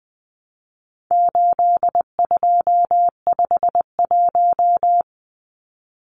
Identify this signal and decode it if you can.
Morse code, heard as a keyed tone, about 20 wpm, 700 Hz. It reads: 8251
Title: Numbers - 4-Digit 20wpm